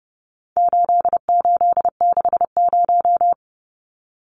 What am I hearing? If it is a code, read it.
Morse code, heard as a keyed tone, about 30 wpm, 700 Hz. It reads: 8860
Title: Numbers - 4-Digit 30wpm